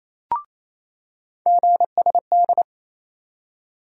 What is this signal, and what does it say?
Morse code, heard as a keyed tone, about 28 wpm, 700 Hz. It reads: GSD
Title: USA Call Sign Postfixes - 3 Letters 28wpm